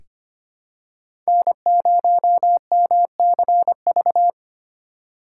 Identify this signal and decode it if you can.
Morse code, heard as a keyed tone, about 25 wpm, 700 Hz. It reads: N0MCV
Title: USA Call Signs - 5 Characters - 4x Character Spacing - 25wpm